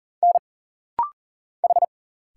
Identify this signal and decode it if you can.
Morse code, heard as a keyed tone, about 40 wpm, 700 Hz. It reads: N H
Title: Single Letters - Rapid-Fire 40wpm